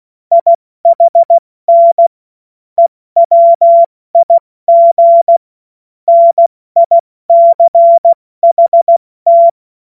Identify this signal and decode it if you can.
Morse code, heard as a keyed tone, about 16 wpm, 680 Hz. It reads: IHN EWIG NICHT
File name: Morsekurs / Witze / Witz 15. Juni 2021, 70 F80